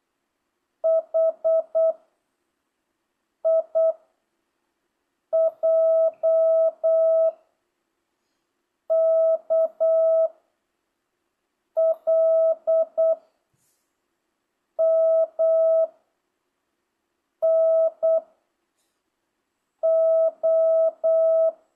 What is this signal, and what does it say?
Morse code, heard as a keyed tone, about 8 wpm, 660 Hz. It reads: H I J K L M N O